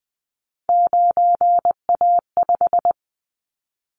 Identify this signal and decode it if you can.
Morse code, heard as a keyed tone, about 20 wpm, 700 Hz. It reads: 9A5
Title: Callsign Prefixes for Top 25 DXCC - 20wpm